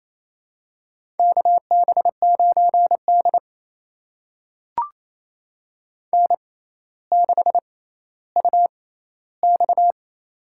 Morse code, heard as a keyed tone, about 28 wpm, spaced as for 14 wpm, 700 Hz. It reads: KB9D N6UX
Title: USA Call Signs - 4 Characters - 6x Character Spacing - 28wpm